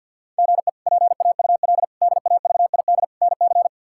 Morse code, heard as a keyed tone, about 49 wpm, 700 Hz. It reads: GE PAUL DAVID NC
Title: K1USN SST QSO Practice - 50wpm